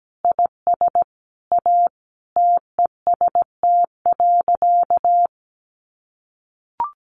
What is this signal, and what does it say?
Morse code, heard as a keyed tone, about 17 wpm, 700 Hz. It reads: IS A TEST.
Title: Additional Easy Phrases 17wpm